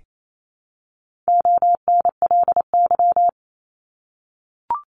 Morse code, heard as a keyed tone, about 28 wpm, 700 Hz. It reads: ONLY
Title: Top 200 Words 28wpm